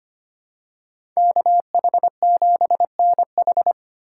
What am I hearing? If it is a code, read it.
Morse code, heard as a keyed tone, about 25 wpm, 700 Hz. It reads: KH7NH
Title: USA Call Signs - 3x Character Spacing - 25wpm